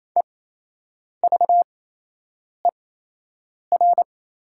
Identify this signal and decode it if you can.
Morse code, heard as a keyed tone, about 28 wpm, spaced as for 8 wpm, 700 Hz. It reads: EVER
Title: 8x Character Spacing - Top 500 Words - 28wpm